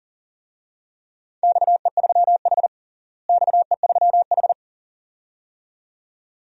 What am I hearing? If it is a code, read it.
Morse code, heard as a keyed tone, about 40 wpm, 700 Hz. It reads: XE3H XE3H
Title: DXCC-100 Call Signs - 4 Characters - Repeated 2x - 40wpm